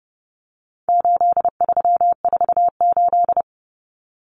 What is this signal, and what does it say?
Morse code, heard as a keyed tone, about 29 wpm, 700 Hz. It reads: 8348
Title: Common Cut Numbers - 4-Digit 30wpm